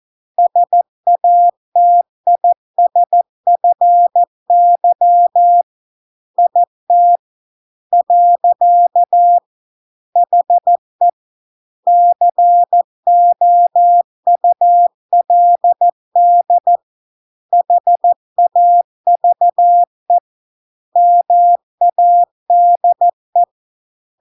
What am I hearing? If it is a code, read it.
Morse code, heard as a keyed tone, about 14 wpm, 700 Hz. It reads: SATISFY IT . HE COULD HAVE MADE